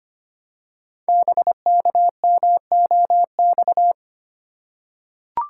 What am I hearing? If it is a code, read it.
Morse code, heard as a keyed tone, about 25 wpm, 700 Hz. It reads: BKMOX
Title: Code Groups - 5 Letters - 25wpm